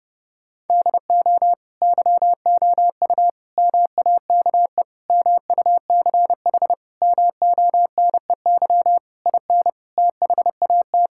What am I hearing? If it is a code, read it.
Morse code, heard as a keyed tone, about 30 wpm, 700 Hz. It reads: DO YOU MAKE MUCH MONEY IN THAT
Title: Sentences from Top 1000 Words - Review Part 6 - 30wpm